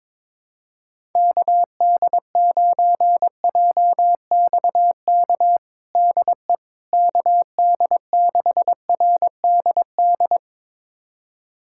Morse code, heard as a keyed tone, about 23 wpm, 700 Hz. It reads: KD9JXK DE KD6RDD
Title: USA Callsign DE Callsign - 6 Characters - 22wpm